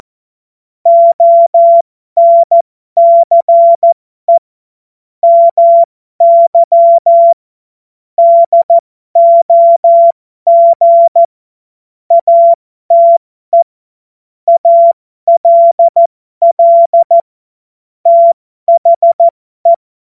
Morse code, heard as a keyed tone, about 14 wpm, spaced as for 12 wpm, 680 Hz. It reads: ONCE MY DOG ATE ALL THE
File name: Morsekurs / Witze